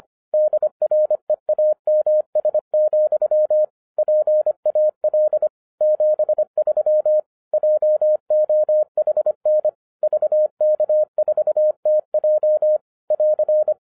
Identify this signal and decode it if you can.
Morse code, heard as a keyed tone, about 26 wpm, 610 Hz. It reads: DREAMS, PAL 73 JOHN VK4TJ <AR>
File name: 2012-01-mp3 / n14-FUN WITH NUMBERS-25wpm